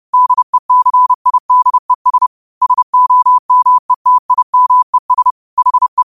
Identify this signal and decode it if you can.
Morse code, heard as a keyed tone, about 30 wpm, 1000 Hz. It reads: DECIDES SOMETIMES HE
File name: qotdfiles / 30wpm